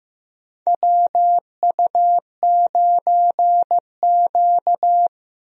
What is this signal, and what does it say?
Morse code, heard as a keyed tone, about 15 wpm, 700 Hz. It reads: WU9Q
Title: USA Call Signs - 4 Characters - Repeated 2x - 15wpm